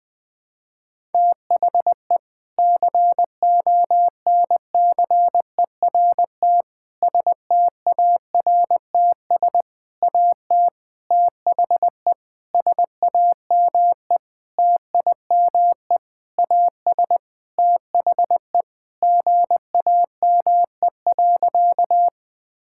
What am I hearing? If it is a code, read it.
Morse code, heard as a keyed tone, about 21 wpm, 700 Hz. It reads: THE CONCERT STARTS AT THE SAME TIME AS THE GAME.
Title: Sentences with 6-Word Phrases - 20wpm